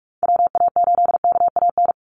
Morse code, heard as a keyed tone, about 44 wpm, 700 Hz. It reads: WA8KAN
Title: Call Signs Repeated 3x - 45wpm